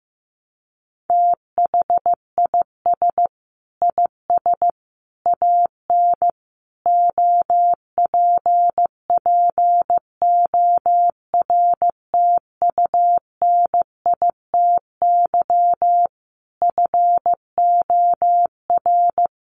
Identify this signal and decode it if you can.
Morse code, heard as a keyed tone, about 15 wpm, 700 Hz. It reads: THIS IS AN OPPORTUNITY FOR
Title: Sentences from Top 1000 Words 15wpm